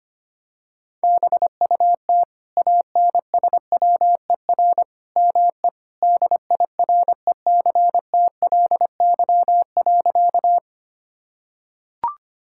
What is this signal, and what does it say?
Morse code, heard as a keyed tone, about 25 wpm, 700 Hz. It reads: BUT ANSWER ME DIRECTLY.
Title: Sentences from Top 1000 Words - Review Part 5 - 25wpm